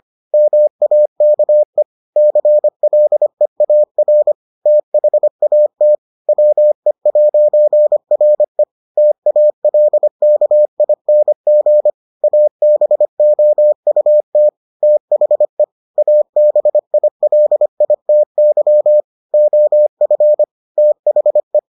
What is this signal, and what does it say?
Morse code, heard as a keyed tone, about 25 wpm, 600 Hz. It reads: MAKE CLEAR THAT WE'RE TALKING ABOUT THE ABILITY OF THE